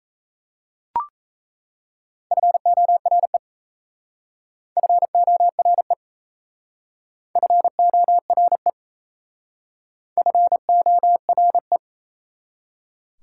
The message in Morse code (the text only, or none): FORE FORE FORE FORE